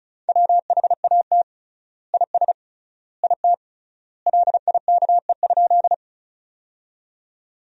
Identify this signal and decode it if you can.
Morse code, heard as a keyed tone, about 36 wpm, 700 Hz. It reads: WHAT IS IT LIKE?
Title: 3x Word-Spacing - Sentences from Top 100 Words - 35wpm